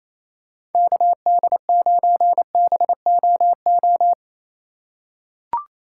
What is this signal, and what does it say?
Morse code, heard as a keyed tone, about 27 wpm, 700 Hz. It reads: KD9BOO
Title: USA Call Signs - 6 Characters - 3x Character Spacing - 28wpm